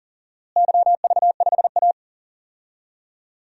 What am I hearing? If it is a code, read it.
Morse code, heard as a keyed tone, about 40 wpm, 700 Hz. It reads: YV5A
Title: DXCC-100 Call Signs - 4 Characters - Repeated 2x - 40wpm